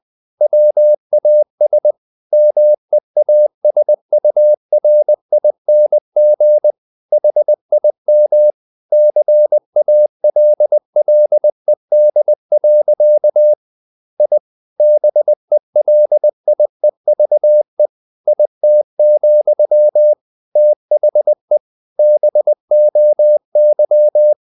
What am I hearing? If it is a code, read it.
Morse code, heard as a keyed tone, about 20 wpm, 600 Hz. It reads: WAS MEASURING HIM CALLED. I BELIEVE IT, THE BOY
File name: the old man and the sea part 2 20wpm